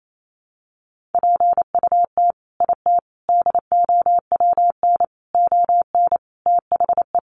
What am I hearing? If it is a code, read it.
Morse code, heard as a keyed tone, about 28 wpm, 700 Hz. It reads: PUT IT DOWN ON THE
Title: Sentences from Top 1400 Words 28wpm